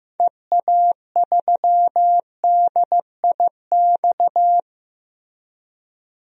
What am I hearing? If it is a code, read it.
Morse code, heard as a keyed tone, about 15 wpm, 700 Hz. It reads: EA3DIX
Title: DXCC-100 Call Signs - 6 Characters - Repeated 3x - 15wpm